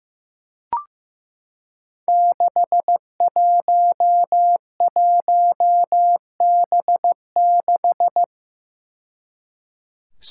Morse code, heard as a keyed tone, about 15 wpm, 700 Hz. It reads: 611B6
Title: Five Challenging Letter-Number Combinations 15wpm